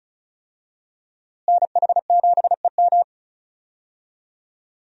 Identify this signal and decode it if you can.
Morse code, heard as a keyed tone, about 36 wpm, 700 Hz. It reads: NH7EM
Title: USA Call Signs - 5 Characters - Repeated 3x - 35wpm